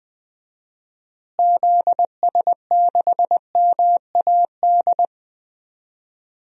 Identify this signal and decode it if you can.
Morse code, heard as a keyed tone, about 19 wpm, 700 Hz. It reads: ZS6MAD
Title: DXCC-100 Call Signs - 6 Characters - Repeated 2x - 20wpm